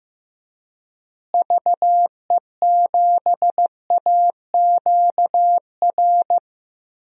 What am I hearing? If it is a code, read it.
Morse code, heard as a keyed tone, about 15 wpm, 700 Hz. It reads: VE7AQR
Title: DXCC-100 Call Signs - 6 Characters - Repeated 2x - 15wpm